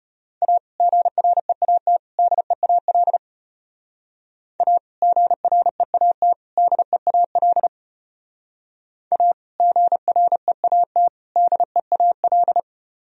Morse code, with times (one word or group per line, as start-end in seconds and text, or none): 0.4-0.6 A
0.8-2.0 GREAT
2.2-3.2 DEAL
4.6-4.8 A
5.0-6.3 GREAT
6.6-7.7 DEAL
9.1-9.3 A
9.6-11.1 GREAT
11.4-12.6 DEAL